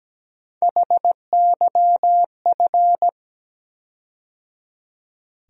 Morse code, heard as a keyed tone, about 17 wpm, 700 Hz. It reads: HYF
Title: USA Call Sign Postfixes - 3 Letters 17wpm